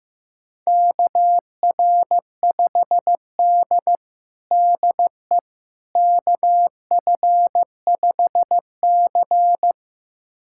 Morse code, heard as a keyed tone, about 15 wpm, 700 Hz. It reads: KR5D DE KF5C
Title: USA Callsign DE Callsign - 4 Characters - 15wpm